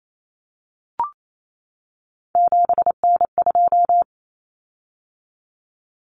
Morse code, heard as a keyed tone, about 28 wpm, 700 Hz. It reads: 7N2
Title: Common Cut Numbers - 3-Digit 28wpm